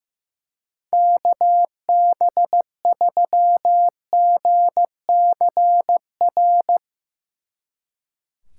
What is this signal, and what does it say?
Morse code, heard as a keyed tone, about 16 wpm, 700 Hz. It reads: KB3GCR
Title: Speed Racing - USA 2x3 Call Signs 15wpm